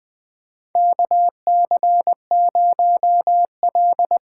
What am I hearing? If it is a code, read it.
Morse code, heard as a keyed tone, about 20 wpm, 700 Hz. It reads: KC0L
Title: USA Call Signs - 6 Characters - 8x Character Spacing - 20wpm